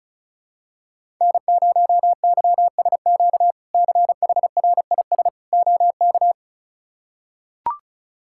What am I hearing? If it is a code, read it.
Morse code, heard as a keyed tone, about 36 wpm, 700 Hz. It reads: N0YSQ CHRIS OK
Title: K1USN SST QSO Practice - 35wpm